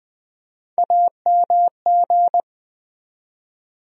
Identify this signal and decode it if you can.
Morse code, heard as a keyed tone, about 20 wpm, 700 Hz. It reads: AMG
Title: USA Call Sign Postfixes - 3 Letters 20wpm